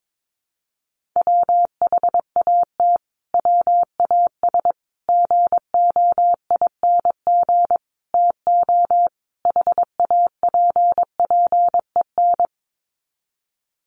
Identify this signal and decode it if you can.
Morse code, heard as a keyed tone, about 22 wpm, 700 Hz. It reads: WHAT WAS GOING TO HAPPEN
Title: Sets of 5 Words Encore 22wpm